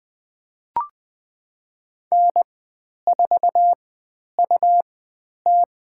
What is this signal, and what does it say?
Morse code, heard as a keyed tone, about 20 wpm, spaced as for 10 wpm, 700 Hz. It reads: N4UT
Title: 109 Lesson 18g - Letter m - Call Signs 20wpm